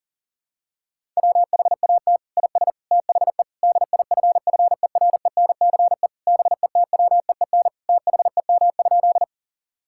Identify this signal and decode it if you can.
Morse code, heard as a keyed tone, about 40 wpm, 700 Hz. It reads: WHAT IS THE DIFFERENCE BETWEEN THEM?
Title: Sentences from Top 1000 Words - Review Part 5 - 40wpm